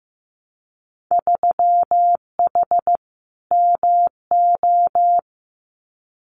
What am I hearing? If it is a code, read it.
Morse code, heard as a keyed tone, about 15 wpm, 700 Hz. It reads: 3H MO
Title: Winter Field Day Exchanges Encore 15wpm